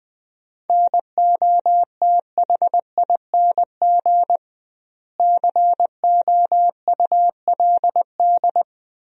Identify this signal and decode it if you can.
Morse code, heard as a keyed tone, about 20 wpm, 700 Hz. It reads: NOTHING COULD
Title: 2x Word-Spacing - Sentences from Top 500 Words - 20wpm